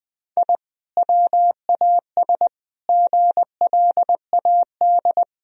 Morse code, heard as a keyed tone, about 19 wpm, 700 Hz. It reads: I WAS GLAD
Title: Sentences from Top 1000 Words - Review Part 1 - 20wpm